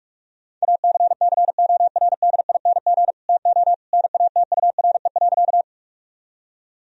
Morse code, heard as a keyed tone, about 46 wpm, 700 Hz. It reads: ACCORDING TO NATURE.
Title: Sentences from Top 1000 Words - Review Part 4 - 45wpm